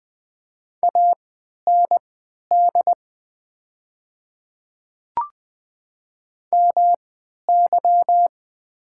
Morse code, heard as a keyed tone, about 20 wpm, spaced as for 11 wpm, 700 Hz. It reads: AND MY
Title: 3x Character Spacing - Top 100 Words - 20wpm